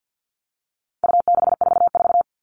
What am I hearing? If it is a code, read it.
Morse code, heard as a keyed tone, about 50 wpm, 700 Hz. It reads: U644
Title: All Cut Numbers - 4-Digit 50wpm